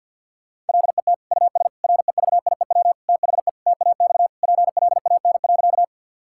Morse code, heard as a keyed tone, about 50 wpm, 700 Hz. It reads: LET US REVIEW THE TAX PLAN.
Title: Sentences from Top 1000 Words - Review Part 6 - 50wpm